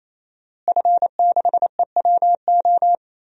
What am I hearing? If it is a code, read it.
Morse code, heard as a keyed tone, about 28 wpm, 700 Hz. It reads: F6EWO
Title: DXCC-100 Call Signs - 5 Characters - Repeated 3x - 28wpm